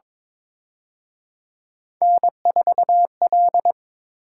Morse code, heard as a keyed tone, about 26 wpm, 700 Hz. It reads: N4L
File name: Speed-Racing - Three Letter-Number Combinations - 22wpm